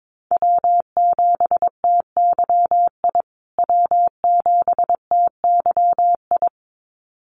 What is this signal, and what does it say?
Morse code, heard as a keyed tone, about 22 wpm, 700 Hz. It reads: W7TYI W7TYI
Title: USA Call Signs Repeated 2x 22wpm